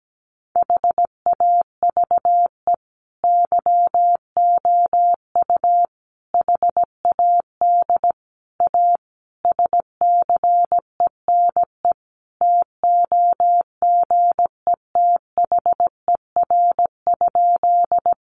Morse code, heard as a keyed tone, about 17 wpm, 700 Hz. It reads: HAVE YOU HAD A SCENE TOGETHER?
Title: Sentences from Top 1100 Words 17wpm